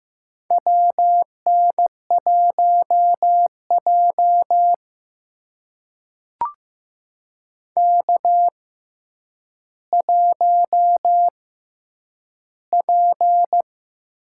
Morse code, heard as a keyed tone, about 15 wpm, spaced as for 8 wpm, 700 Hz. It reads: WN1J K1P
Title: USA Call Signs - 6x Character Spacing - 15wpm